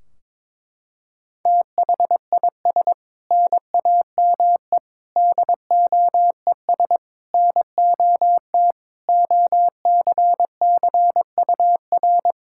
Morse code, heard as a keyed tone, about 21 wpm, 700 Hz. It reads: THIS NAME DOES NOT OCCUR
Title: Sentences from Top 1600 Words 22wpm